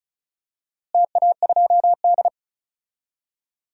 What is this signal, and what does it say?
Morse code, heard as a keyed tone, about 36 wpm, 700 Hz. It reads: TA2D
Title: DXCC-100 Call Signs - 4 Characters - Repeated 2x - 35wpm